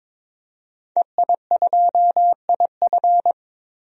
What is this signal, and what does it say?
Morse code, heard as a keyed tone, about 22 wpm, 700 Hz. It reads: EI2IF
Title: DXCC-100 Call Signs - 5 Characters - 22wpm